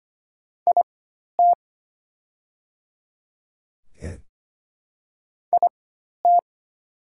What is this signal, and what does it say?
Morse code, heard as a keyed tone, about 25 wpm, spaced as for 12 wpm, 700 Hz. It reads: IT IT
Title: 4x Character Spacing - Top 200 Words - 25wpm